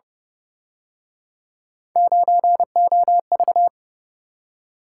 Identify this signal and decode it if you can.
Morse code, heard as a keyed tone, about 31 wpm, 700 Hz. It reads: 9OV
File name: Speed-Racing - Three Letter-Number Combinations - 25wpm